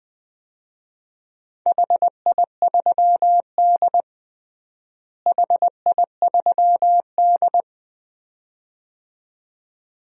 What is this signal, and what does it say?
Morse code, heard as a keyed tone, about 20 wpm, 700 Hz. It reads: HI3D HI3D
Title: DXCC-100 Call Signs - 4 Characters - Repeated 2x - 20wpm